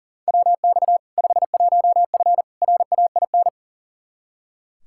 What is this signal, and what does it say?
Morse code, heard as a keyed tone, about 40 wpm, 700 Hz. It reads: WX 51F RAIN
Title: K1USN SST QSO Practice 40wpm